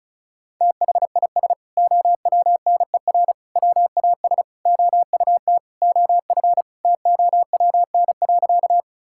Morse code, heard as a keyed tone, about 36 wpm, 700 Hz. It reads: THIS OWNER WAS OUT OF TOWN.
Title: Sentences from Top 1600 Words 35wpm